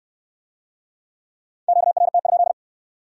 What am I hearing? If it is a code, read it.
Morse code, heard as a keyed tone, about 63 wpm, 700 Hz. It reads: QRT?